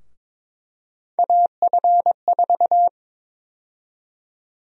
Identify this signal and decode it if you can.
Morse code, heard as a keyed tone, about 22 wpm, 700 Hz. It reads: AF4
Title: USA Call Sign Prefixes - Repeated 2x - 22wpm